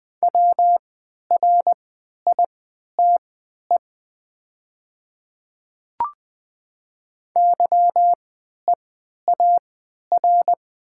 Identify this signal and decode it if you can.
Morse code, heard as a keyed tone, about 20 wpm, spaced as for 11 wpm, 700 Hz. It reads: WRITE YEAR
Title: 3x Character Spacing - Top 200 Words - 20wpm